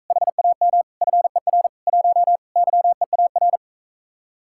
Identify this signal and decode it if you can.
Morse code, heard as a keyed tone, about 45 wpm, 700 Hz. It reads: HAM FER 1 YEAR